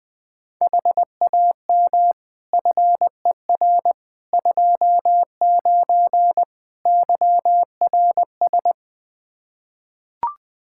Morse code, heard as a keyed tone, about 20 wpm, 700 Hz. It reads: HAM FER 29 YRS
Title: Practice QSO - Exchange 3 - More Info - 20wpm